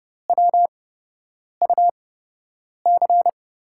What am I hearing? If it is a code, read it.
Morse code, heard as a keyed tone, about 30 wpm, spaced as for 8 wpm, 700 Hz. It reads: WUC
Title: USA Call Signs - 8x Character Spacing - 30wpm